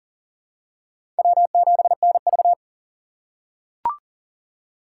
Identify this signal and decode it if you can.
Morse code, heard as a keyed tone, about 40 wpm, 700 Hz. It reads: W7NV
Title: USA Call Signs - 4 Characters - 3x Character Spacing - 40wpm